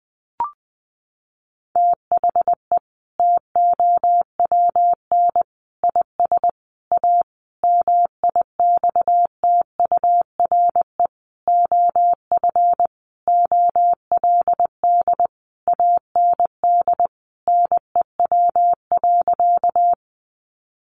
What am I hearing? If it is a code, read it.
Morse code, heard as a keyed tone, about 20 wpm, 700 Hz. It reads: THE TOWN IS A MIXTURE OF OLD AND NEW.
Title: Sentences from Top 1500 Words 20wpm